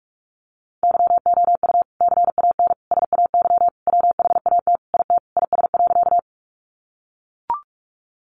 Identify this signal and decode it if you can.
Morse code, heard as a keyed tone, about 44 wpm, 700 Hz. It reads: YOU CAN SAY WHAT IT IS.
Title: Sentences from Top 200 Words 45wpm